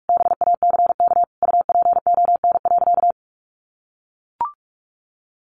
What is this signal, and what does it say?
Morse code, heard as a keyed tone, about 44 wpm, 700 Hz. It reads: BACK UPON.
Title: Sentences from Top 1100 Words 45wpm